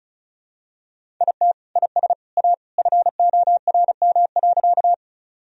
Morse code, heard as a keyed tone, about 34 wpm, 700 Hz. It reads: IT IS A FORM.
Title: Sentences from Top 1000 Words - Review Part 1 - 35wpm